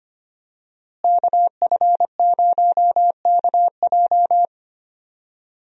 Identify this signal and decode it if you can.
Morse code, heard as a keyed tone, about 25 wpm, 700 Hz. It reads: KF0KJ
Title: USA Call Signs - 5 Characters - Repeated 2x - 25wpm